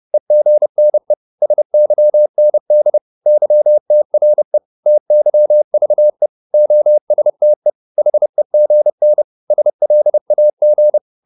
Morse code, heard as a keyed tone, about 30 wpm, 600 Hz. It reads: EGNE SYND YTRE TYVE OSTE HEGN SLAG
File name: Korte ord DK 30wpm